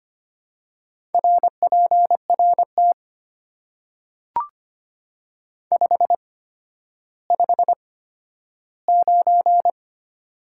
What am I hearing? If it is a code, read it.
Morse code, heard as a keyed tone, about 25 wpm, spaced as for 12 wpm, 690 Hz. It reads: RPRT 559
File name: Top 90 QSO Elements - 8x Character Spacing - 25wpm